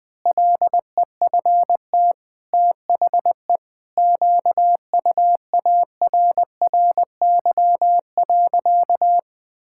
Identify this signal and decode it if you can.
Morse code, heard as a keyed tone, about 20 wpm, 700 Hz. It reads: LEFT THE QUARRY.
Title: Pangram Challenge v2 - 20wpm